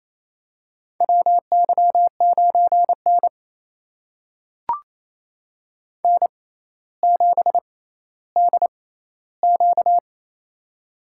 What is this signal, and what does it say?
Morse code, heard as a keyed tone, about 28 wpm, spaced as for 14 wpm, 700 Hz. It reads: WY9N N7DQ
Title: USA Call Signs - 4 Characters - 6x Character Spacing - 28wpm